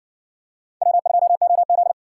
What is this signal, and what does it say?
Morse code, heard as a keyed tone, about 60 wpm, 700 Hz. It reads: W2OZ